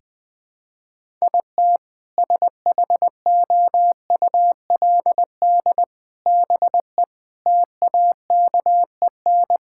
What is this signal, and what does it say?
Morse code, heard as a keyed tone, about 20 wpm, 700 Hz. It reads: IT SHOULD BE TAKEN
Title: Sentences with 8-Word Phrases - 20wpm